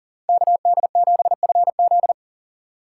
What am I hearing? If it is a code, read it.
Morse code, heard as a keyed tone, about 38 wpm, 700 Hz. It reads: KD7FZ
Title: USA Call Signs - 5 Characters - 4x Character Spacing - 40wpm